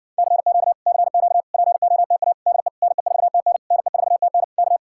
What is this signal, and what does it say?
Morse code, heard as a keyed tone, about 60 wpm, 700 Hz. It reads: CQ CQ POTA DE NE4TN NE4TN K